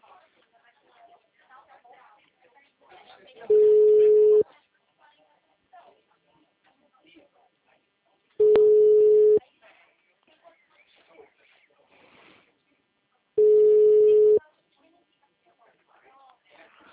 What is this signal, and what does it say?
Ringback tone (Europe, 425 Hz), with speech.